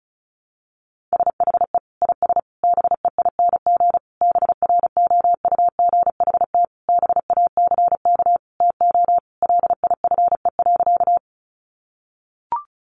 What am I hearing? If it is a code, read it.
Morse code, heard as a keyed tone, about 34 wpm, 700 Hz. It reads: SHE IS BEING BROUGHT BACK TO LIFE.
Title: Sentences from Top 600 Words 35wpm